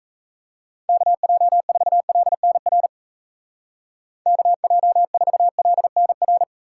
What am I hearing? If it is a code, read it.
Morse code, heard as a keyed tone, about 33 wpm, 700 Hz. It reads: KJ4LNR KJ4LNR
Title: Speed Racing - USA 2x3 Call Signs 28wpm